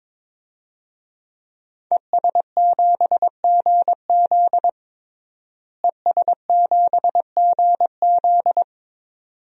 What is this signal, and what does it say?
Morse code, heard as a keyed tone, about 23 wpm, 700 Hz. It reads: ES7GZ ES7GZ
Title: DXCC-100 Call Signs - 5 Characters - Repeated 2x - 22wpm